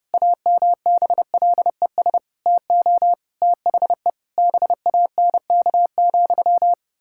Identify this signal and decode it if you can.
Morse code, heard as a keyed tone, about 30 wpm, 700 Hz. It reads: AMBLES TO THE BANK,
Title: Pangram Challenge v2 - 30wpm